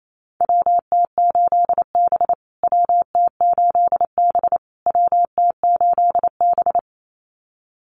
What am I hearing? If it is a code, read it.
Morse code, heard as a keyed tone, about 28 wpm, 700 Hz. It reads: WT8B WT8B WT8B
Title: USA Call Signs Repeated 3x 28wpm